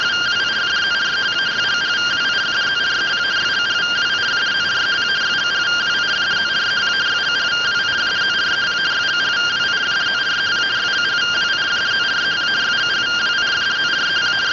Two-tone teleprinter signal, about 170 Hz shift.
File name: findings / CT 205 ax mips, iDose 3 0006.gen-16 rtty rx